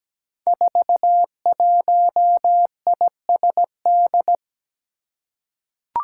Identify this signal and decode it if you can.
Morse code, heard as a keyed tone, about 17 wpm, 700 Hz. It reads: 41ISD
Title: Five Challenging Letter-Number Combinations 17wpm